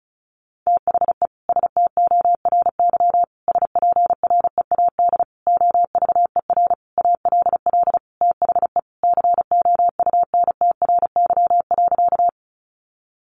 Morse code, heard as a keyed tone, about 34 wpm, 700 Hz. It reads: THE STORY SPREAD OVER ALL THE COUNTRY.
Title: Sentences from Top 1200 Words 35wpm